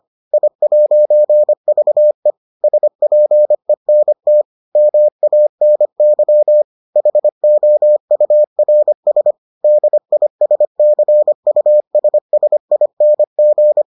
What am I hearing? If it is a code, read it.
Morse code, heard as a keyed tone, about 25 wpm, 600 Hz. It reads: I'VE SPENT MANY HOURS DISCUSSING